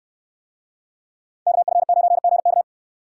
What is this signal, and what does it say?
Morse code, heard as a keyed tone, about 68 wpm, 700 Hz. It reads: KF9GK